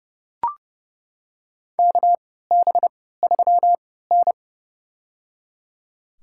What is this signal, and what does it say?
Morse code, heard as a keyed tone, about 30 wpm, spaced as for 17 wpm, 700 Hz. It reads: KB3N
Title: USA Call Signs - 4 Characters - 3x Character Spacing - 30wpm